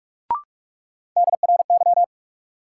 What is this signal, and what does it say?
Morse code, heard as a keyed tone, about 46 wpm, 700 Hz. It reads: DRY
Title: Sentences from Top 500 Words - New Word Drill - 45wpm